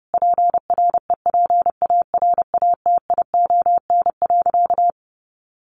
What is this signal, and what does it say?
Morse code, heard as a keyed tone, about 30 wpm, 700 Hz. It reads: PREPARATION.
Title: Sentences from Top 1400 Words 30wpm